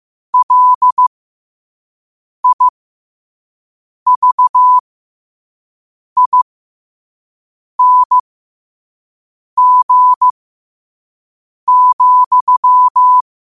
Morse code, heard as a keyed tone, about 15 wpm, spaced as for 5 wpm, 1000 Hz. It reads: LIVING,
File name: qotdfiles / 5wpm